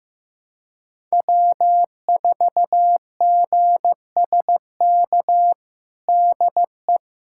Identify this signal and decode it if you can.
Morse code, heard as a keyed tone, about 15 wpm, 700 Hz. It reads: W4GSK DE
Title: USA Callsign DE Callsign 15wpm